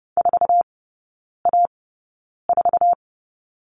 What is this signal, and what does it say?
Morse code, heard as a keyed tone, about 30 wpm, spaced as for 9 wpm, 700 Hz. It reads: 4A4
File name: Lesson 18e - Letter m - Three Characters 30wpm